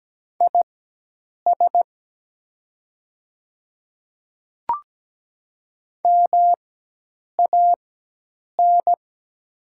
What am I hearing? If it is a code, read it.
Morse code, heard as a keyed tone, about 17 wpm, spaced as for 8 wpm, 700 Hz. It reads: IS MAN
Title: 4x Character Spacing - Top 300 Words - 17wpm